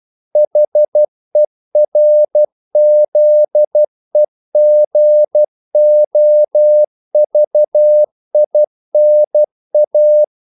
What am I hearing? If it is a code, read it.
Morse code, heard as a keyed tone, about 12 wpm, 600 Hz. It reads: HERZEGOVINA
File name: CW / Practice / NationalCapitals12